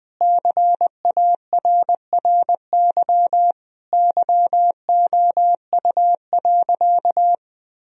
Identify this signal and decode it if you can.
Morse code, heard as a keyed tone, about 21 wpm, 700 Hz. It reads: CARRY YOU.
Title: Sentences from Top 1000 Words - Review Part 1 - 20wpm